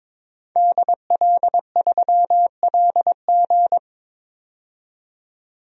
Morse code, heard as a keyed tone, about 22 wpm, 700 Hz. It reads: DL3LG
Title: DXCC-100 Call Signs - 5 Characters - 22wpm